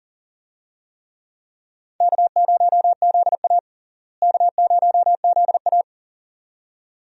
Morse code, heard as a keyed tone, about 40 wpm, 700 Hz. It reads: K0ZA K0ZA
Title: USA Call Signs - 4 Characters - Repeated 2x - 40wpm